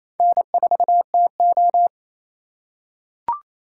Morse code, heard as a keyed tone, about 28 wpm, 700 Hz. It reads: N4TO
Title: USA Call Signs - 4 Characters - 8x Character Spacing - 28wpm